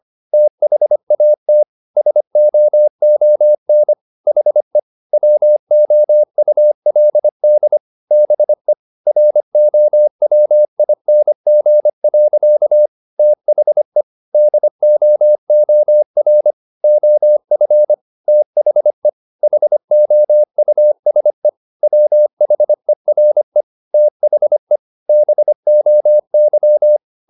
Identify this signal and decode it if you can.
Morse code, heard as a keyed tone, about 25 wpm, 600 Hz. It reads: THAT SOON HE WOULD BE ROWING. THE DOOR OF THE HOUSE WHERE THE BOY
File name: the old man and the sea part 1 25wpm